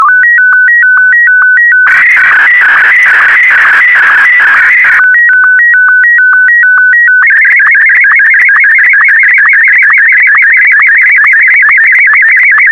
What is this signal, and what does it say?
SSTV (slow-scan television), Martin 1.
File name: sstv / M1-AX4MSL